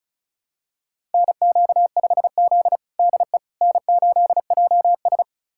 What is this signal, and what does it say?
Morse code, heard as a keyed tone, about 36 wpm, 700 Hz. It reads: NQ5Z DE N8JS
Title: USA Callsign DE Callsign - 4 Characters - 35wpm